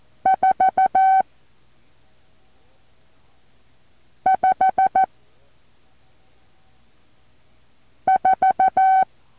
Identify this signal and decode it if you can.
Morse code, heard as a keyed tone, about 14 wpm, 760 Hz. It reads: 4 5 4
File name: intl morse code2